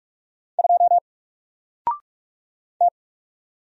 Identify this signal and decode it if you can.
Morse code, heard as a keyed tone, about 44 wpm, 700 Hz. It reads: 2 T
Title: Single Character (Letter:Number:Punctuation:Prosign) Alt 45wpm